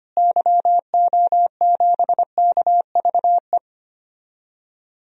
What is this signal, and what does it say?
Morse code, heard as a keyed tone, about 25 wpm, 700 Hz. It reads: YO7KVE
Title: DXCC-100 Call Signs - 6 Characters - Repeated 3x - 25wpm